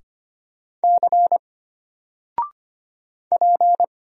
Morse code, heard as a keyed tone, about 25 wpm, 700 Hz. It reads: C P
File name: Single-Letter-Number-25wpm